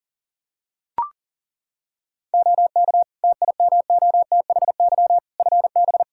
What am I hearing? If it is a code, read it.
Morse code, heard as a keyed tone, about 40 wpm, 700 Hz. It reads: OK TIMOTHY FB
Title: Practice QSO - Exchange 2 - Info - 40wpm